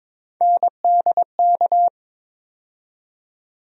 Morse code, heard as a keyed tone, about 22 wpm, 700 Hz. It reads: NDK
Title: USA Call Sign Postfixes - 3 Letters 22wpm